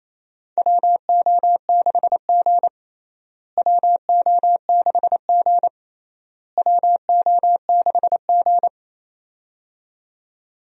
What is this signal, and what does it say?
Morse code, heard as a keyed tone, about 27 wpm, 700 Hz. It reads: WO6G WO6G WO6G
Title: USA Call Signs - 4 Characters - Repeated 3x - 28wpm